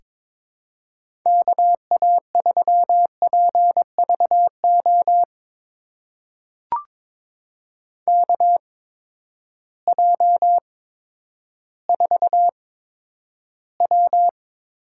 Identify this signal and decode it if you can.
Morse code, heard as a keyed tone, about 22 wpm, spaced as for 11 wpm, 700 Hz. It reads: KA3PVO KJ4W
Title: USA Call Signs - 6 Characters - 8x Character Spacing - 22wpm